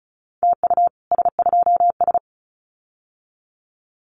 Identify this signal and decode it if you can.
Morse code, heard as a keyed tone, about 34 wpm, 700 Hz. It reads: TU S2S
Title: Common QSO Phrases Encore 35wpm